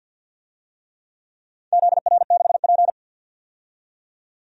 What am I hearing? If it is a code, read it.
Morse code, heard as a keyed tone, about 53 wpm, 700 Hz. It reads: ZR6P
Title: DXCC-100 Call Signs - 4 Characters - 50wpm